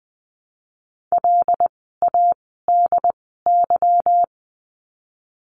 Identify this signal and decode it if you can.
Morse code, heard as a keyed tone, about 20 wpm, spaced as for 15 wpm, 700 Hz. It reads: LADY
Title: 146 Lesson 24f - Letter y - Words 20wpm